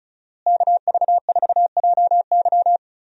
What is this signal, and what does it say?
Morse code, heard as a keyed tone, about 35 wpm, 700 Hz. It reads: KV4JY
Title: USA Call Signs - 5 Characters - Repeated 3x - 35wpm